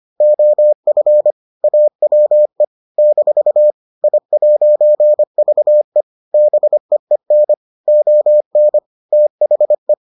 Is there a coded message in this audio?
Yes: Morse code from 0.2 to 9.9 s, heard as a keyed tone, about 25 wpm, 600 Hz. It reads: OF AWE - I'VE BEEN ON THE